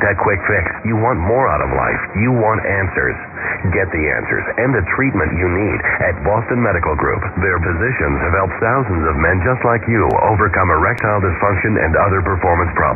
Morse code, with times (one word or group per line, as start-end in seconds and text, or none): none